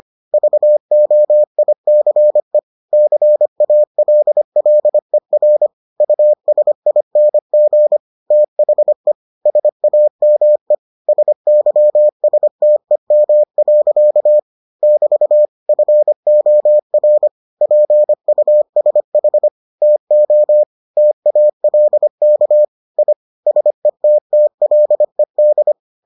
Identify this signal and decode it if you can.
Morse code, heard as a keyed tone, about 25 wpm, 600 Hz. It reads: VOICE CALLER USING THE SAME SYSTEM. <BT> FOR PUSH TO TALK I SETTLED